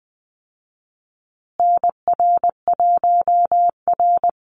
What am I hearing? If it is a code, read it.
Morse code, heard as a keyed tone, about 20 wpm, 700 Hz. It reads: NR1R
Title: Call Signs Repeated 3x - 20wpm